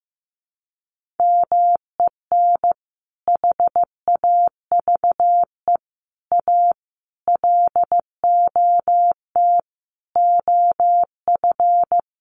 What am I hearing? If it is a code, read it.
Morse code, heard as a keyed tone, about 15 wpm, 700 Hz. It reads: MEN HAVE A LOT OF
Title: Sentences from Top 1400 Words 15wpm